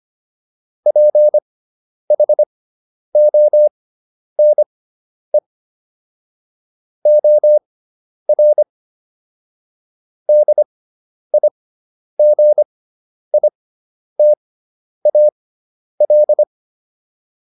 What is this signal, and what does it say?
Morse code, heard as a keyed tone, about 25 wpm, spaced as for 10 wpm, 600 Hz. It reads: PHONE OR DIGITAL